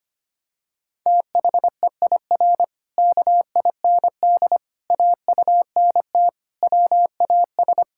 Morse code, heard as a keyed tone, about 25 wpm, 700 Hz. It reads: THEIR KIND AUNT WAS
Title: Pangram Challenge v2 - 25wpm